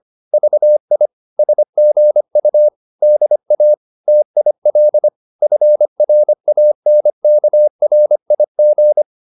Morse code, heard as a keyed tone, about 25 wpm, 600 Hz. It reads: VI SGU DA TIL FRANKRIG
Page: Ferie 25 wpm